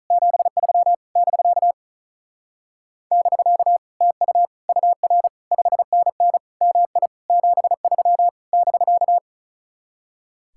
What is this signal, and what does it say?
Morse code, heard as a keyed tone, about 45 wpm, 700 Hz. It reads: 73 <BK> <BK> TU UR 5NN MI 73 <BK>